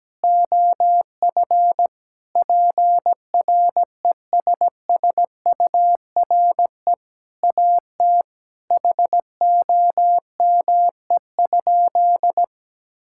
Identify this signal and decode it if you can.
Morse code, heard as a keyed tone, about 17 wpm, 700 Hz. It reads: OF PRESSURE AT HOME?
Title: Sentences from Top 1000 Words - Review Part 6 - 17wpm